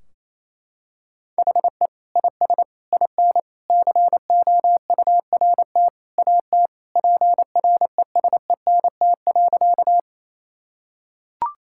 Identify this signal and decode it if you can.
Morse code, heard as a keyed tone, about 28 wpm, 700 Hz. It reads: HE IS IN COURT AT PRESENT.
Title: Sentences from Top 1600 Words 28wpm